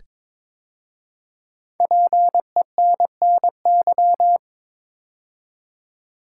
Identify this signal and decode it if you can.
Morse code, heard as a keyed tone, about 22 wpm, 700 Hz. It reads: PENNY
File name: 5 Letter Words - First 3 Letters Given - 22wpm